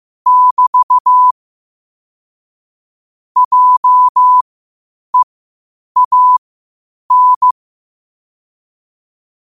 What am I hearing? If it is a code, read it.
Morse code, heard as a keyed tone, about 15 wpm, spaced as for 8 wpm, 1000 Hz. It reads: <BT> JEAN
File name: qotdfiles / 8wpm